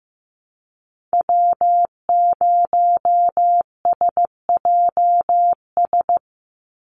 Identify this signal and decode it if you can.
Morse code, heard as a keyed tone, about 15 wpm, 700 Hz. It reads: W0SJS
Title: Call Signs Repeated 2x - 15wpm